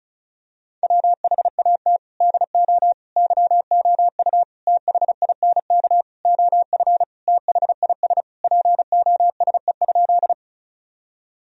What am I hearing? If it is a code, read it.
Morse code, heard as a keyed tone, about 36 wpm, 700 Hz. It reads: WHAT DO YOU THINK OF THIS POSE?
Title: Sentences from Top 1000 Words - Review Part 2 - 35wpm